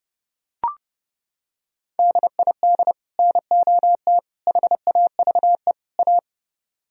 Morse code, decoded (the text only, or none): DID NOT HAVE A